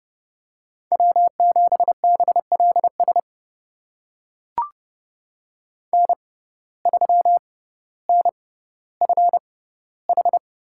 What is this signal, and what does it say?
Morse code, heard as a keyed tone, about 30 wpm, spaced as for 15 wpm, 700 Hz. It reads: W7BLS N3NFH
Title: USA Call Signs - 5 Characters - 6x Character Spacing - 30wpm